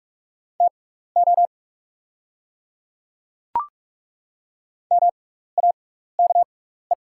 Morse code, heard as a keyed tone, about 48 wpm, spaced as for 16 wpm, 690 Hz. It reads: TO MAKE
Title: 6x Character Spacing - Top 100 Words - 45wpm